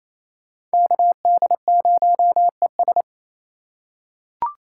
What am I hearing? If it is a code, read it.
Morse code, heard as a keyed tone, about 28 wpm, 690 Hz. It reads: KD0ES
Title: USA Call Signs - 5 Characters - 4x Character Spacing - 28wpm